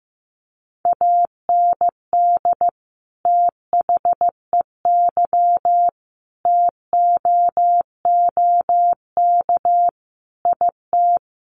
Morse code, heard as a keyed tone, about 15 wpm, 700 Hz. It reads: AND THEY TOOK IT
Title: Sentences from Top 1400 Words 15wpm